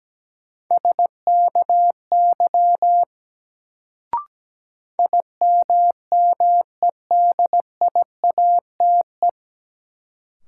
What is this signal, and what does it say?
Morse code, heard as a keyed tone, about 17 wpm, 700 Hz. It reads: SKY IMMEDIATE
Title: Sentences from Top 1200 Words - New Word Drill - 17wpm